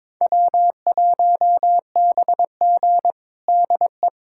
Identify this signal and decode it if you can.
Morse code, heard as a keyed tone, about 22 wpm, 700 Hz. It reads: W1BG DE
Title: USA Callsign DE Callsign - 4 Characters - 22wpm